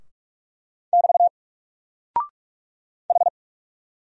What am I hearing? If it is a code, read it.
Morse code, heard as a keyed tone, about 44 wpm, 700 Hz. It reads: <BT> H
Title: Single Character (Letter:Number:Punctuation:Prosign) 45wpm